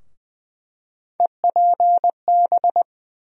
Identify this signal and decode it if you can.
Morse code, heard as a keyed tone, about 20 wpm, 700 Hz. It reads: EPB
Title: USA Call Sign Postfixes - 3 Letters 20wpm